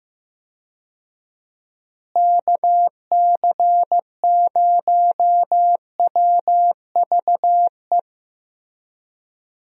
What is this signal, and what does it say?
Morse code, heard as a keyed tone, about 15 wpm, 700 Hz. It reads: KC0WVE
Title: USA Call Signs - 6 Characters - Repeated 2x - 15wpm